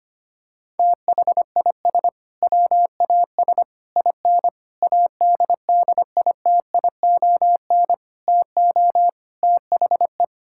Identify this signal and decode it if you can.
Morse code, heard as a keyed tone, about 25 wpm, 700 Hz. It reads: THIS WAS IN ADDITION TO THE
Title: Sentences from Top 1000 Words - Review Part 6 - 25wpm